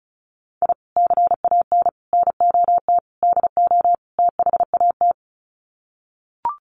Morse code, heard as a keyed tone, about 34 wpm, 700 Hz. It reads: I CAN NOT DO THAT
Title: Sets of 5 Words Encore2 35wpm